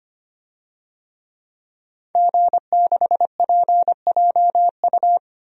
Morse code, heard as a keyed tone, about 25 wpm, 700 Hz. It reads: G6PJU
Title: DXCC-100 Call Signs - 5 Characters - 25wpm